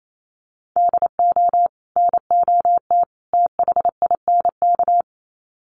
Morse code, heard as a keyed tone, about 28 wpm, 700 Hz. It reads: DO NOT THINK
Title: Sets of 3 Words 28wpm